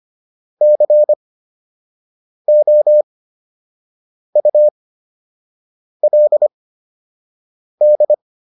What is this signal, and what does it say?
Morse code, heard as a keyed tone, about 25 wpm, spaced as for 6 wpm, 600 Hz. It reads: COULD